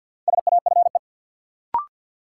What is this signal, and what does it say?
Morse code, heard as a keyed tone, about 49 wpm, 700 Hz. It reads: SAFE
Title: Sentences from Top 900 Words - New Word Drill - 50wpm